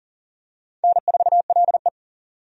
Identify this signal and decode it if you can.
Morse code, heard as a keyed tone, about 42 wpm, 690 Hz. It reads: N4LE
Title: USA Call Signs - 4 Characters - 40wpm